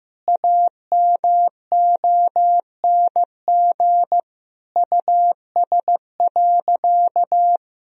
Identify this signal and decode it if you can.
Morse code, heard as a keyed tone, about 15 wpm, 700 Hz. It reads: AMONG US.
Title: Sentences from Top 1000 Words - Review Part 3 - 15wpm